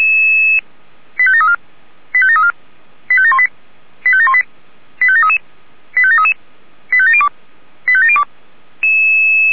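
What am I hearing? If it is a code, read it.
Five-tone selective call (ZVEI), heard as quick runs of five tones, tones of about 70 ms. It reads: 85734 85734 85718 85718 85733 85733 85702 85702
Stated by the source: ZVEI five-tone selective call 85860 received with an SDR.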